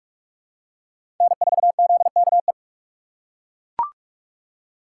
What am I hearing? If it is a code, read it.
Morse code, heard as a keyed tone, about 42 wpm, 700 Hz. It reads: N4ZKE
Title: USA Call Signs - 3x Character Spacing - 45wpm